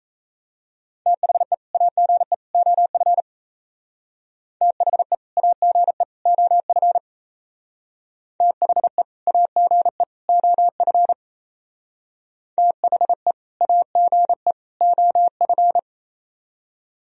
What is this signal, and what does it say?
Morse code, heard as a keyed tone, about 37 wpm, 700 Hz. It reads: THE AGE OF THE AGE OF THE AGE OF THE AGE OF